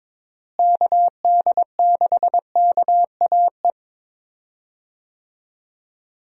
Morse code, heard as a keyed tone, about 22 wpm, 700 Hz. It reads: KD6KAE
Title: USA Call Signs - 6 Characters - Repeated 2x - 22wpm